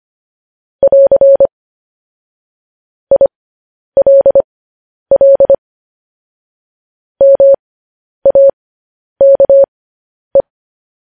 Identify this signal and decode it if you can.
Morse code, heard as a keyed tone, about 25 wpm, spaced as for 10 wpm, 550 Hz. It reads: <AR> ILL MAKE
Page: news-headlines-in-morse-code-at-10-wpm